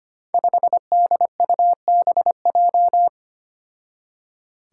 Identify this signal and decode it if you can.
Morse code, heard as a keyed tone, about 25 wpm, 700 Hz. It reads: 5DUBJ
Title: Five Challenging Letter-Number Combinations 25wpm